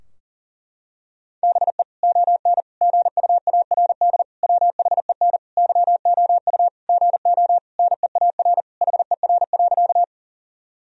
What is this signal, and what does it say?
Morse code, heard as a keyed tone, about 40 wpm, 700 Hz. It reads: BE ON GUARD WHEN YOU GO NEAR HER.
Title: Sentences from Top 1000 Words - Review Part 4 - 40wpm